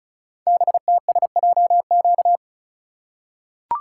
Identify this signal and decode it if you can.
Morse code, heard as a keyed tone, about 36 wpm, 700 Hz. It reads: BTSJQ
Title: Code Groups - 5 Letters - 35wpm